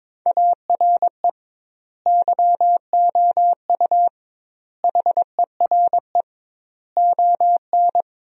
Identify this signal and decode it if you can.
Morse code, heard as a keyed tone, about 22 wpm, 700 Hz. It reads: ARE YOU HERE ON
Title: 2x Word-Spacing - Sentences from Top 400 Words - 22wpm